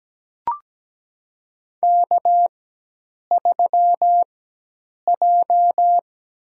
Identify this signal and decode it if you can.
Morse code, heard as a keyed tone, about 17 wpm, spaced as for 8 wpm, 700 Hz. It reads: K3J
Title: USA Call Signs - 5 Characters - 4x Character Spacing - 17wpm